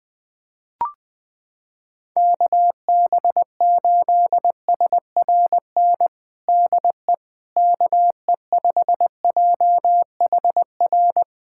Morse code, heard as a keyed tone, about 20 wpm, 700 Hz. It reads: KB8SRN DE KE5JHR
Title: USA Callsign DE Callsign - 6 Characters - 20wpm